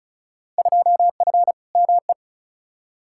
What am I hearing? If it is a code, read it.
Morse code, heard as a keyed tone, about 36 wpm, 700 Hz. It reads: 2F ME
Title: Field Day Exchanges - 35wpm